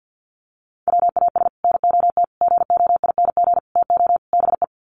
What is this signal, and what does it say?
Morse code, heard as a keyed tone, about 50 wpm, 700 Hz. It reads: WAS NOT GOING TO BE
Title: Sets of 5 Words Encore 50wpm